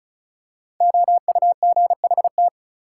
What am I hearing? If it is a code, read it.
Morse code, heard as a keyed tone, about 36 wpm, 700 Hz. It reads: OUGHT
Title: Five Letter Words - 35wpm